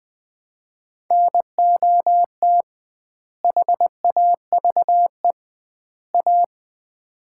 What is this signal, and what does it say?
Morse code, heard as a keyed tone, about 20 wpm, 700 Hz. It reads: NOT HAVE A
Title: 2x Word-Spacing - Sets of 3 Words - 20wpm